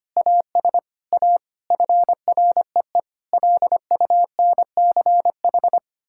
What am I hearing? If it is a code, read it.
Morse code, heard as a keyed tone, about 25 wpm, 700 Hz. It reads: AS A FREE LUNCH
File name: Sets of 9 Words - 25wpm